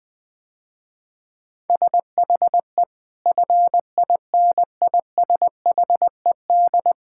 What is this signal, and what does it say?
Morse code, heard as a keyed tone, about 21 wpm, 700 Hz. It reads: SHE FINISHED
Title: Sentences with 8-Word Phrases - Spoken Beginning - 20wpm